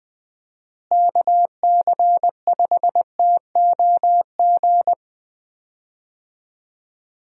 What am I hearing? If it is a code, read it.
Morse code, heard as a keyed tone, about 20 wpm, 700 Hz. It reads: KC5TOG
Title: USA Call Signs - 6 Characters - Repeated 2x - 20wpm